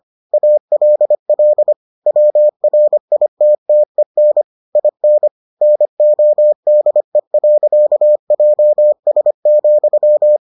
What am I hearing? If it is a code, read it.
Morse code, heard as a keyed tone, about 25 wpm, 600 Hz. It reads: ALL WRITTEN IN NODE.JS,